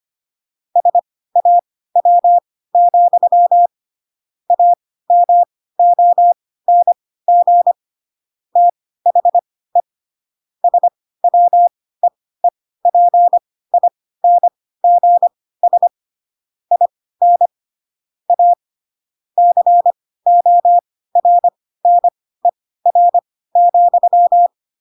Morse code, heard as a keyed tone, about 25 wpm, spaced as for 16 wpm, 700 Hz. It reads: SAW, AMONG THE SWEEPINGS IN A CORNER,
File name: Pinocchio / Pinocchio ch5 10 25 ik1hge